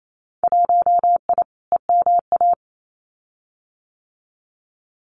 Morse code, heard as a keyed tone, about 28 wpm, 700 Hz. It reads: 1I EMA
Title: Winter Field Day Exchanges Encore 28wpm